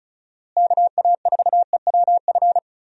Morse code, heard as a keyed tone, about 34 wpm, 700 Hz. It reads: KA4EWF
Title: USA Call Signs - 6 Characters - Repeated 3x - 35wpm